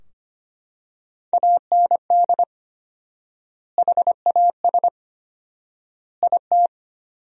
Morse code, heard as a keyed tone, about 25 wpm, 700 Hz. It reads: AND HAS IT
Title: 4x Word-Spacing - Sentences from Top 100 Words - 25wpm